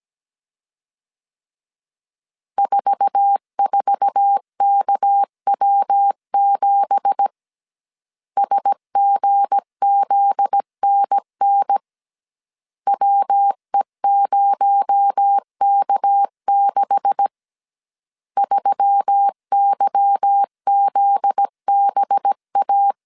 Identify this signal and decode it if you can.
Morse code, heard as a keyed tone, about 17 wpm, 800 Hz. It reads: 44KW7 SGZNN WE0K6 3YZBA
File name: CW / DET / 1 9 1500C S C 16wpm 2909